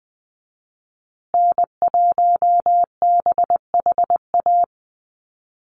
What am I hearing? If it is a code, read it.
Morse code, heard as a keyed tone, about 20 wpm, 700 Hz. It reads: N1BHA
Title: USA Call Signs Repeated 3x 20wpm